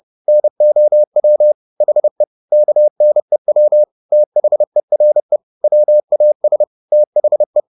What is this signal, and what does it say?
Morse code, heard as a keyed tone, about 30 wpm, 600 Hz. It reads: NOW HE KNEW THERE WAS THE